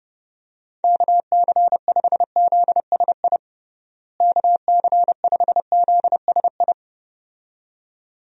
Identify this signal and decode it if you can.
Morse code, heard as a keyed tone, about 30 wpm, 700 Hz. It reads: KC5ZSI KC5ZSI
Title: USA Call Signs - 6 Characters - Repeated 2x - 30wpm